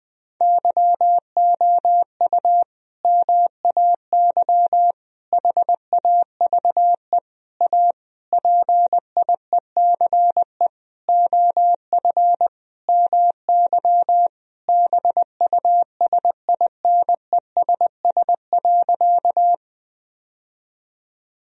Morse code, heard as a keyed tone, about 21 wpm, 700 Hz. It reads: YOU MAY HAVE A PIECE OF MY BUSINESS.
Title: Sentences from Top 1000 Words - Review Part 2 - 20wpm